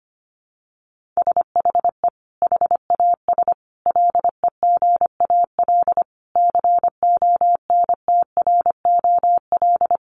Morse code, heard as a keyed tone, about 25 wpm, 700 Hz. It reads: SHE HAS LEGAL CONTROL
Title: Sentences from Top 1200 Words 25wpm